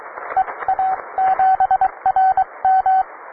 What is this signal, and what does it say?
Morse code, heard as a keyed tone, about 23 wpm, spaced as for 19 wpm, 760 Hz, with radio noise. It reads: EA7RM